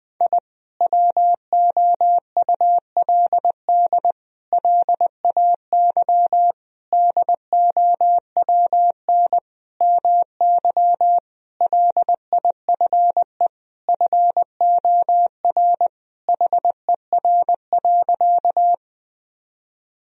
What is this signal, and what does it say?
Morse code, heard as a keyed tone, about 20 wpm, 700 Hz. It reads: I WOULD LAY DOWN MY LIFE FOR HER.
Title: Sentences from Top 1000 Words - Review Part 2 - 20wpm